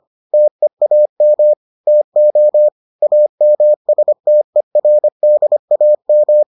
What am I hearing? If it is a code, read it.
Morse code, heard as a keyed tone, about 25 wpm, 600 Hz. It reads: TEAM TO AMSTERDAM